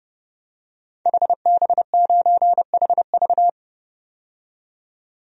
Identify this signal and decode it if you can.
Morse code, heard as a keyed tone, about 32 wpm, 700 Hz. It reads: HB9HV
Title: DXCC-100 Call Signs - 5 Characters - 30wpm